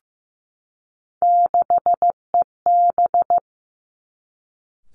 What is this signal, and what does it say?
Morse code, heard as a keyed tone, about 15 wpm, 700 Hz. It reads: 6EB
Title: Three Challenging Letter-Number Combinations 15wpm